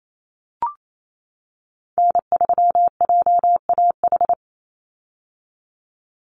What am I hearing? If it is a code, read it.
Morse code, heard as a keyed tone, about 28 wpm, 700 Hz. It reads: N3JAH
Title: USA Call Signs 28wpm